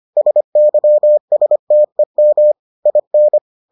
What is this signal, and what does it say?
Morse code, heard as a keyed tone, about 25 wpm, 600 Hz. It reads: SYSTEM IN